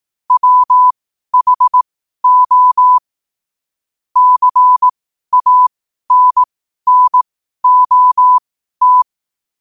Morse code, heard as a keyed tone, about 18 wpm, spaced as for 12 wpm, 1000 Hz. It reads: WHO CANNOT
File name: qotdfiles / 12wpm